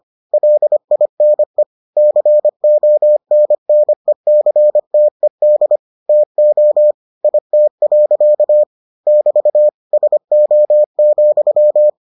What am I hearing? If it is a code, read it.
Morse code, heard as a keyed tone, about 25 wpm, 600 Hz. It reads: LINE CONNECTED TO IT. <BT> SO,